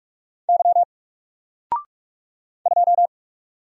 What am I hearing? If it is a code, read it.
Morse code, heard as a keyed tone, about 44 wpm, 700 Hz. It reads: Y 2
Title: Single Character (Letter:Number:Punctuation:Prosign) 45wpm